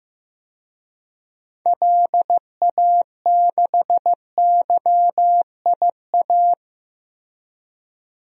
Morse code, heard as a keyed tone, about 15 wpm, 700 Hz. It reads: LA6YIA
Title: DXCC-100 Call Signs - 6 Characters - Repeated 3x - 15wpm